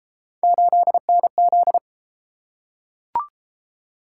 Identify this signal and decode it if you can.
Morse code, heard as a keyed tone, about 32 wpm, 700 Hz. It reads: 8NZ